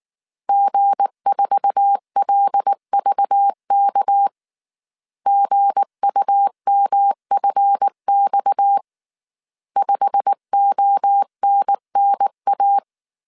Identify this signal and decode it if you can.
Morse code, heard as a keyed tone, about 19 wpm, 800 Hz. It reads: G4LVK GUMFX 5ONNA
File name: CW / DET / 1 9 1500C S A 18wpm 2459